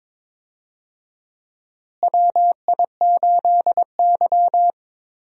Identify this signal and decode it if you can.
Morse code, heard as a keyed tone, about 22 wpm, 700 Hz. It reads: WI8Y
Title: USA Call Signs - 4 Characters - Repeated 3x - 22wpm